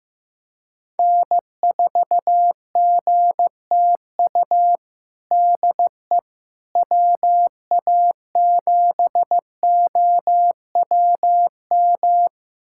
Morse code, heard as a keyed tone, about 15 wpm, 700 Hz. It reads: N4GTU DE WA7OWM
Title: USA Callsign DE Callsign 15wpm